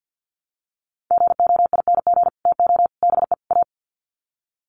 Morse code, heard as a keyed tone, about 50 wpm, 700 Hz. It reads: GOING TO BE A
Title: Sets of 4 Words 50wpm